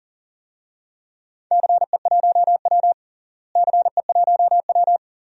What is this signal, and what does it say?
Morse code, heard as a keyed tone, about 40 wpm, 700 Hz. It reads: CE1W CE1W
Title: DXCC-100 Call Signs - 4 Characters - Repeated 2x - 40wpm